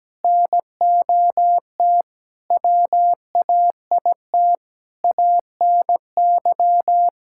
Morse code, heard as a keyed tone, about 17 wpm, 700 Hz. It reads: NOT WAIT ANY
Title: Sentences from Top 1000 Words - Review Part 5 - 17wpm